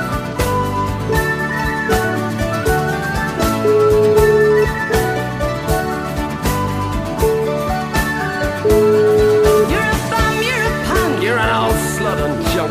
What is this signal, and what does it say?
Ringback tone (Europe, 425 Hz), with noise.